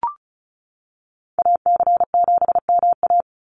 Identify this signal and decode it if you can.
Morse code, heard as a keyed tone, about 34 wpm, 700 Hz. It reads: AC7MA
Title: NA SOTA Activator Callsigns 35wpm